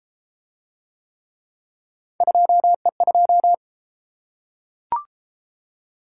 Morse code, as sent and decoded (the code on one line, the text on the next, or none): ..--- . ..---
2E2